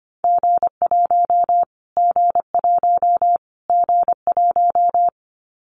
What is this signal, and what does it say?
Morse code, heard as a keyed tone, about 25 wpm, 700 Hz. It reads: G1 G1 G1
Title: Callsign Prefixes for Top 25 DXCC - Repeated 3x - 25wpm